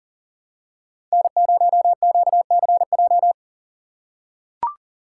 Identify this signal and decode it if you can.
Morse code, heard as a keyed tone, about 40 wpm, 700 Hz. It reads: N0QCJ
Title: USA Call Signs - 5 Characters - 3x Character Spacing - 40wpm